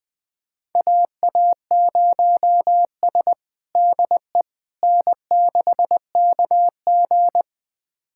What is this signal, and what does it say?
Morse code, heard as a keyed tone, about 21 wpm, 700 Hz. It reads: AA0S DE N6KG
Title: USA Callsign DE Callsign - 4 Characters - 20wpm